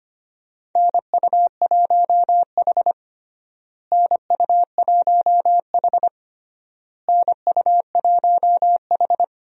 Morse code, heard as a keyed tone, about 25 wpm, 700 Hz. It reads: NU1H NU1H NU1H
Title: USA Call Signs - 4 Characters - Repeated 3x - 25wpm